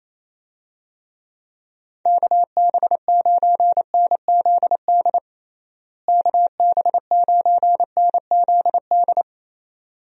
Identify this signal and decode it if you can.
Morse code, heard as a keyed tone, about 28 wpm, 700 Hz. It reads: KB9NZD KB9NZD
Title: USA Call Signs - 6 Characters - Repeated 2x - 28wpm